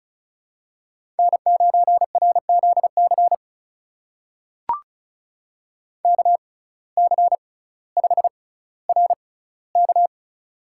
Morse code, heard as a keyed tone, about 36 wpm, spaced as for 18 wpm, 700 Hz. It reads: N9RZC KC5RK
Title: USA Call Signs - 5 Characters - 6x Character Spacing - 35wpm